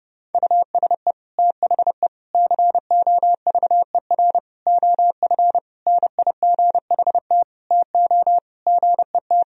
Morse code, heard as a keyed tone, about 30 wpm, 700 Hz. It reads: USE THE COVER OF NIGHT TO GET
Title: Sentences from Top 1000 Words - Review Part 1 - 30wpm